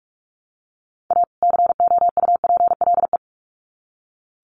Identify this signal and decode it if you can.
Morse code, heard as a keyed tone, about 44 wpm, 700 Hz. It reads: A COUPLE
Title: Sets of 2 Words 45wpm